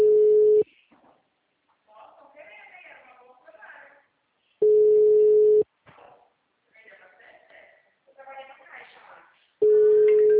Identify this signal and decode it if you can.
Ringback tone (Europe, 425 Hz), with speech.
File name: out- -2013- - - .8